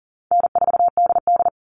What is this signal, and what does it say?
Morse code, heard as a keyed tone, about 40 wpm, 700 Hz. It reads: N4DD
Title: Easy Call Signs Repeated 3x - 40wpm